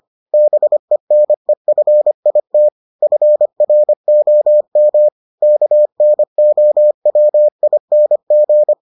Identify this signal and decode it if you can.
Morse code, heard as a keyed tone, about 25 wpm, 600 Hz. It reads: BENEFIT FROM KNOWING